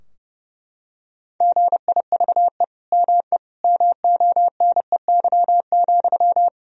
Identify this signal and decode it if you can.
Morse code, heard as a keyed tone, about 30 wpm, 700 Hz. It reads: GIVE ME MONEY,
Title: Sentences from Top 1000 Words - Review Part 2 - 30wpm